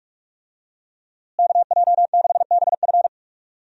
Morse code, heard as a keyed tone, about 46 wpm, 700 Hz. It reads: KJ6BF
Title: USA Call Signs - 5 Characters - Repeated 2x - 45wpm